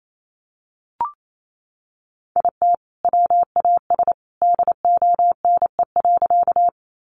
Morse code, heard as a keyed tone, about 28 wpm, 700 Hz. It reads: IT WAS DONE.
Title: Sentences from Top 500 Words 28wpm